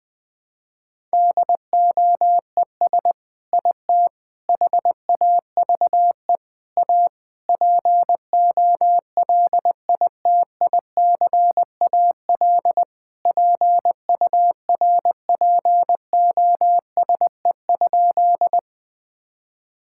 Morse code, heard as a keyed tone, about 20 wpm, 700 Hz. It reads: DOES IT HAVE A POLITICAL PURPOSE?
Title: Sentences from Top 1000 Words - Review Part 5 - 20wpm